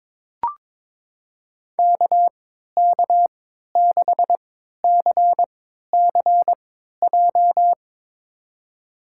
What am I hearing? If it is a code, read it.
Morse code, heard as a keyed tone, about 22 wpm, spaced as for 13 wpm, 700 Hz. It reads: KK6CCJ
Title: USA Call Signs - 6 Characters - 3x Character Spacing - 22wpm